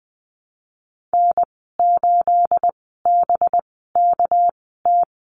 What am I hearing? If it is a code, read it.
Morse code, heard as a keyed tone, about 20 wpm, spaced as for 14 wpm, 700 Hz. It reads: N8BKT
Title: 219 Lesson 35g - Number 0 - Call Signs 20wpm